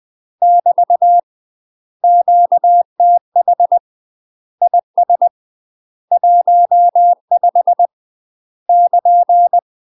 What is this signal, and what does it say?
Morse code, heard as a keyed tone, about 20 wpm, 700 Hz. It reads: <BT> QTH IS 15 <KN>
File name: UK MASTER QSOs TRAINING / 20WPM / UK MASTER QSOs 0008